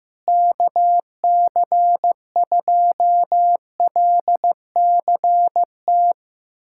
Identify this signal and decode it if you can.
Morse code, heard as a keyed tone, about 15 wpm, 700 Hz. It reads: KC2LCT
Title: USA Callsign DE Callsign 15wpm